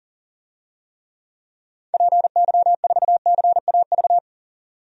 Morse code, heard as a keyed tone, about 40 wpm, 700 Hz. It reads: PY4CAV
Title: DXCC-100 Call Signs - 6 Characters - 40wpm